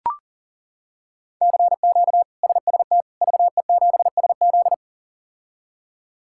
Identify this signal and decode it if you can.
Morse code, heard as a keyed tone, about 40 wpm, 690 Hz. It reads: CQ SST VE7SZ
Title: K1USN SST QSO Practice - 40wpm